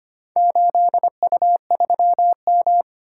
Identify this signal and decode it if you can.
Morse code, heard as a keyed tone, about 25 wpm, 700 Hz. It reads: 8U3M
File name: practice / 25wpm / Four Letter-Number Combinations - 25wpm